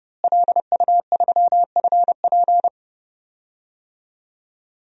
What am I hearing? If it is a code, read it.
Morse code, heard as a keyed tone, about 32 wpm, 700 Hz. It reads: LU3FP
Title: DXCC-100 Call Signs - 5 Characters - Repeated 2x - 30wpm